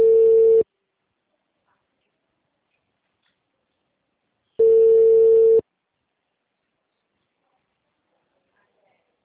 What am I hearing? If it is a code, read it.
Ringback tone (China, 450 Hz).
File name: out- -2014- - - .2